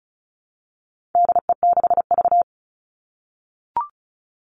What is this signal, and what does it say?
Morse code, heard as a keyed tone, about 34 wpm, 700 Hz. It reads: DE6V
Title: All Cut Numbers - 4-Digit 35wpm